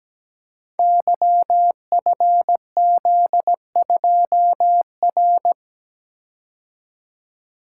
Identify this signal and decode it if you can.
Morse code, heard as a keyed tone, about 17 wpm, 700 Hz. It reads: YFZ2R
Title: Code Groups - 5 Letter:Numbers - 17wpm